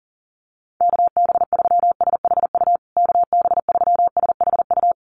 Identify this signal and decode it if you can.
Morse code, heard as a keyed tone, about 40 wpm, 700 Hz. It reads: KB3SHU KB3SHU
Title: USA Call Signs Repeated 3x 40wpm